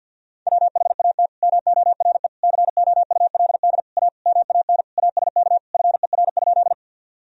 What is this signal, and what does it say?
Morse code, heard as a keyed tone, about 53 wpm, 700 Hz. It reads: WHAT MORE COULD A MAN ASK FER?
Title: Sentences from Top 200 Words and Abbreviations 50wpm